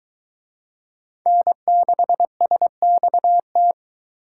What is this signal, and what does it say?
Morse code, heard as a keyed tone, about 22 wpm, 700 Hz. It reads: N6SXT